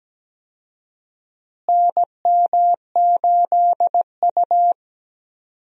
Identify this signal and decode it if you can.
Morse code, heard as a keyed tone, about 17 wpm, 700 Hz. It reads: NM8U
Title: USA Call Signs - 4 Characters - Repeated 2x - 17wpm